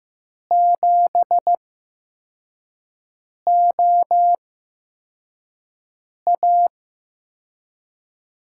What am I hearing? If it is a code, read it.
Morse code, heard as a keyed tone, about 15 wpm, spaced as for 4 wpm, 700 Hz. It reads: 7OA
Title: USA Call Signs - 5 Characters - 8x Character Spacing - 15wpm